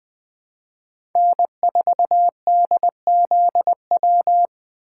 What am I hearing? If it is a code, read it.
Morse code, heard as a keyed tone, about 20 wpm, 700 Hz. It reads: N4DZW
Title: USA Call Signs - 2x Character Spacing - 20wpm